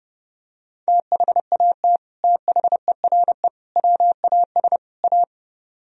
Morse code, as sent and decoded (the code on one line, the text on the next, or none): - .... .- - / - .... .-. / .-- .- ... / .-
THAT THERE WAS A